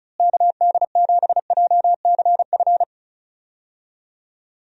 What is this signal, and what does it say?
Morse code, heard as a keyed tone, about 34 wpm, 700 Hz. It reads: KD7JCF
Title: USA Call Signs - 6 Characters - Repeated 2x - 35wpm